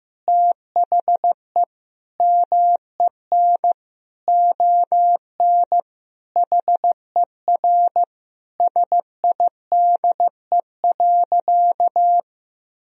Morse code, heard as a keyed tone, about 14 wpm, 700 Hz. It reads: THE MEN ON HER SIDE.
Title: Sentences from Top 1000 Words - Review Part 1 - 15wpm